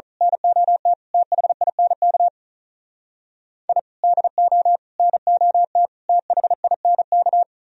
Morse code, heard as a keyed tone, about 38 wpm, 700 Hz. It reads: NOT THINK I DO NOT THINK
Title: Speed-Racing - Sets of 4 Words 35wpm